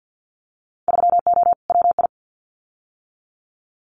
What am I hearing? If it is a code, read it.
Morse code, heard as a keyed tone, about 50 wpm, 700 Hz. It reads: 3O WI
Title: Winter Field Day Exchanges 50wpm